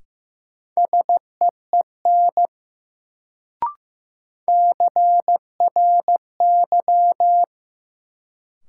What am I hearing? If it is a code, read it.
Morse code, heard as a keyed tone, about 15 wpm, 700 Hz. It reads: SEEN CRY
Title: Sets of 5 Words Encore - New Word Drill - 15wpm